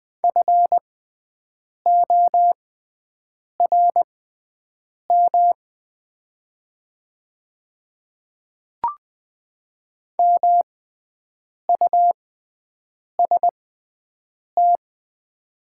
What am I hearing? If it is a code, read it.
Morse code, heard as a keyed tone, about 20 wpm, spaced as for 7 wpm, 700 Hz. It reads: FORM MUST
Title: 6x Character Spacing - Top 200 Words - 20wpm